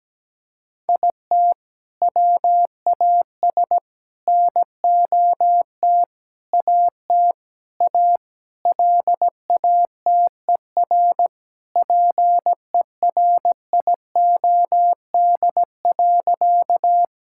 Morse code, heard as a keyed tone, about 17 wpm, 700 Hz. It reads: IT WAS NOT AT A LATER PERIOD.
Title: Sentences from Top 1000 Words - Review Part 5 - 17wpm